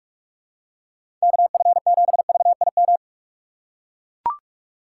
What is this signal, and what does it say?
Morse code, heard as a keyed tone, about 41 wpm, 700 Hz. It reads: KF7VIM
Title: USA Call Signs - 6 Characters - 4x Character Spacing - 45wpm